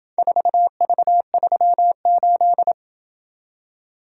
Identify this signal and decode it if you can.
Morse code, heard as a keyed tone, about 26 wpm, 700 Hz. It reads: 4V38